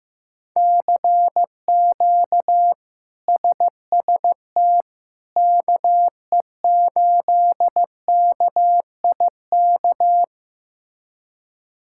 Morse code, heard as a keyed tone, about 15 wpm, 700 Hz. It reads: CQ SST KE8KIK
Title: K1USN SST QSO Practice - 15wpm